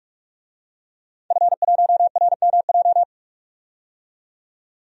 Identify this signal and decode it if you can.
Morse code, heard as a keyed tone, about 42 wpm, 700 Hz. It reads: F1RMJ
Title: DXCC-100 Call Signs - 5 Characters - 45wpm